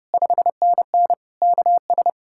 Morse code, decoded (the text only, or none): 5NN KS